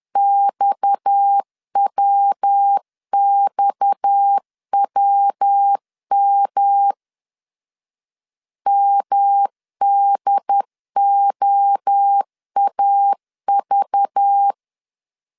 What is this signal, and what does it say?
Morse code, heard as a keyed tone, about 10 wpm, 800 Hz. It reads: XWXWM MDOAV
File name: CW / DET / 1 6 1500C S B 10wpm 4137